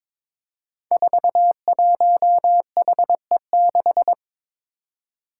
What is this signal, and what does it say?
Morse code, heard as a keyed tone, about 22 wpm, 700 Hz. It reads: 41HE6
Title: Five Challenging Letter-Number Combinations 22wpm